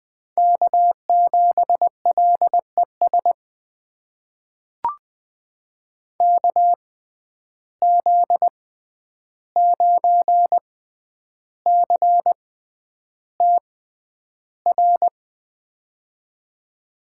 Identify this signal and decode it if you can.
Morse code, heard as a keyed tone, about 20 wpm, spaced as for 10 wpm, 700 Hz. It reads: K7LES KZ9CTR
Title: USA Call Signs - 6x Character Spacing - 20wpm